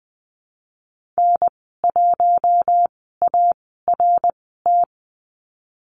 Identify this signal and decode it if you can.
Morse code, heard as a keyed tone, about 20 wpm, spaced as for 15 wpm, 700 Hz. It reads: N1ART
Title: 055 Lesson 10g - Letter r - Call Signs 20wpm